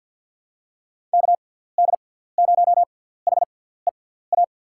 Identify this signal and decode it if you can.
Morse code, heard as a keyed tone, about 50 wpm, spaced as for 17 wpm, 700 Hz. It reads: KD0HEA
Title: USA Call Signs - 6x Character Spacing - 50wpm